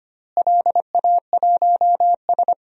Morse code, heard as a keyed tone, about 25 wpm, 700 Hz. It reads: LA1S
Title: DXCC-100 Call Signs - 4 Characters - Repeated 2x - 25wpm